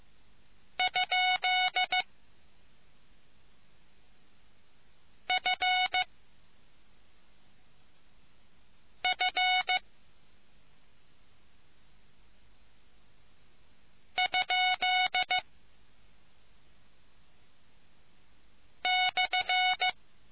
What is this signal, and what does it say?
Morse code, heard as a keyed tone, about 15 wpm, 740 Hz. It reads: ? F F ? /